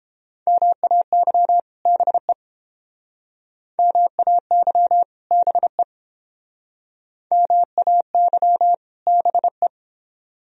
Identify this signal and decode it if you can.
Morse code, heard as a keyed tone, about 31 wpm, 700 Hz. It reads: MAY BE MAY BE MAY BE